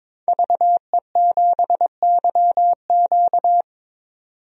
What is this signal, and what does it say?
Morse code, heard as a keyed tone, about 22 wpm, 700 Hz. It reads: VE7YQ
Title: DXCC-100 Call Signs - 5 Characters - 22wpm